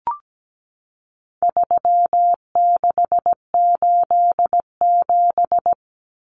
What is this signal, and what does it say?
Morse code, heard as a keyed tone, about 17 wpm, 700 Hz. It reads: 3687
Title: Numbers - 4-Digit 17wpm